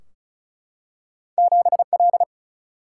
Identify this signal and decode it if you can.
Morse code, heard as a keyed tone, about 36 wpm, 700 Hz. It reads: 7L
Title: Single-Letter Single-Number 35wpm